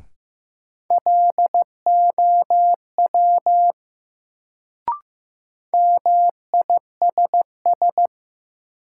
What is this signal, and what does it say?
Morse code, heard as a keyed tone, about 15 wpm, 700 Hz. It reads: LOW MISS
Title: Sentences from Top 500 Words - New Word Drill - 15wpm